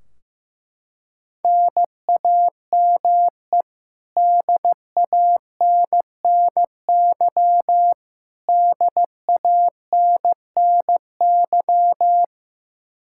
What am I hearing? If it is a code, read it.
Morse code, heard as a keyed tone, about 15 wpm, 700 Hz. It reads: NAME DANNY DANNY
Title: Practice QSO - Exchange 1 - Beginning - 15wpm